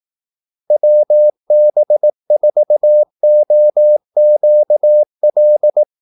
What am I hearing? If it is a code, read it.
Morse code, heard as a keyed tone, about 18 wpm, 600 Hz. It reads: WB4OQL
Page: Kaldesignaler 18 wpm